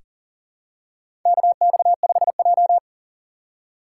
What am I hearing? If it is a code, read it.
Morse code, heard as a keyed tone, about 40 wpm, 700 Hz. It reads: KX5J
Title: USA Call Signs - 6x Character Spacing - 40wpm